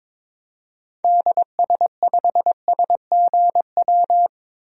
Morse code, heard as a keyed tone, about 22 wpm, 700 Hz. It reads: DS5SGW
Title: DXCC-100 Call Signs - 6 Characters - Repeated 3x - 22wpm